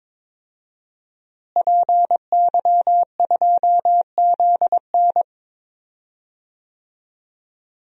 Morse code, heard as a keyed tone, about 22 wpm, 700 Hz. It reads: PY2ZN
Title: DXCC-100 Call Signs - 5 Characters - Repeated 3x - 22wpm